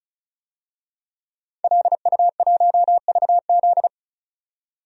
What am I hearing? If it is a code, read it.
Morse code, heard as a keyed tone, about 36 wpm, 700 Hz. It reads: LU1VZ
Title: DXCC-100 Call Signs - 5 Characters - 35wpm